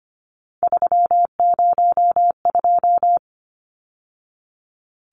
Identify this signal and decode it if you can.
Morse code, heard as a keyed tone, about 25 wpm, 700 Hz. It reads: 302
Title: Numbers - 3-Digit 25wpm